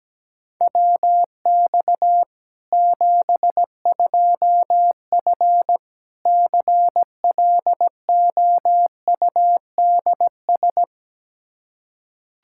Morse code, heard as a keyed tone, about 17 wpm, 700 Hz. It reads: WX 72F CLOUDS
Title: K1USN SST QSO Practice 17wpm